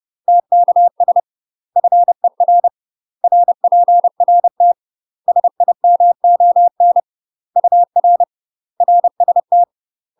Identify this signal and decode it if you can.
Morse code, heard as a keyed tone, about 30 wpm, 700 Hz. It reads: TKS FER RPRT SIMON UR RST
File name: UK MASTER QSOs TRAINING / 30WPM / UK MASTER QSOs 0030